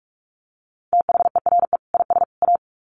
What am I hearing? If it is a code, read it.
Morse code, heard as a keyed tone, about 44 wpm, 700 Hz. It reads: THERE IS A
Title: Sets of 3 Words 45wpm